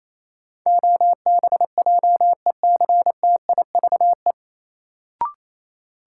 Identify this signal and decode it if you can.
Morse code, heard as a keyed tone, about 28 wpm, 700 Hz. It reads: OBJECTIVE
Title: Sentences from Top 1200 Words - New Word Drill - 28wpm